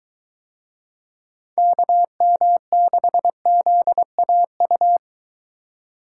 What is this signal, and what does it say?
Morse code, heard as a keyed tone, about 22 wpm, 700 Hz. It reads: KM6ZAU